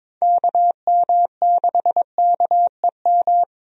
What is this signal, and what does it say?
Morse code, heard as a keyed tone, about 22 wpm, 700 Hz. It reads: KM6KEM
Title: USA Call Signs - 6 Characters - Repeated 2x - 22wpm